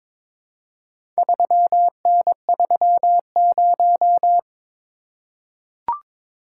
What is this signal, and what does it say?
Morse code, heard as a keyed tone, about 22 wpm, 700 Hz. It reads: 3N30
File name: practice / 22wpm / Four Letter-Number Combinations - 22wpm